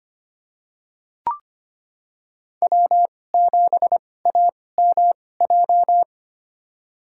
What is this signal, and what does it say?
Morse code, heard as a keyed tone, about 25 wpm, spaced as for 18 wpm, 700 Hz. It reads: W7AMJ
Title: USA Call Signs - 5 Characters - 2x Character Spacing - 25wpm